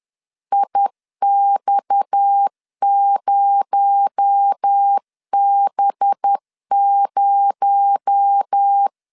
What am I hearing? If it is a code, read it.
Morse code, heard as a keyed tone, about 10 wpm, 800 Hz. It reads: IX0B0
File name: LEZIONI / 6 PKX0 / 6 PKX0 10wpm 2707